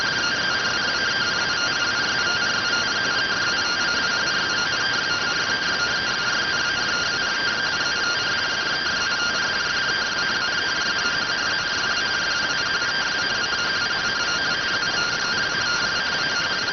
Two-tone teleprinter signal, about 170 Hz shift.